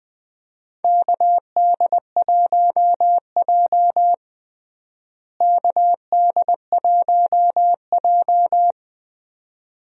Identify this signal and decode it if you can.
Morse code, heard as a keyed tone, about 21 wpm, 700 Hz. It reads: KD1J KD1J
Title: USA Call Signs - 4 Characters - Repeated 2x - 20wpm